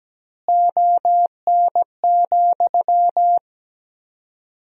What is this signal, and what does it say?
Morse code, heard as a keyed tone, about 17 wpm, 700 Hz. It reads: ON,
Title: 4x Word-Spacing - Sentences from Top 500 Words - 17wpm